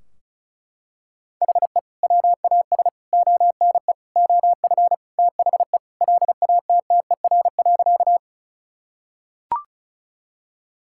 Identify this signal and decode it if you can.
Morse code, heard as a keyed tone, about 34 wpm, 700 Hz. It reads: HE WAS ONE OF THE LATTER.
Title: Sentences from Top 1600 Words 35wpm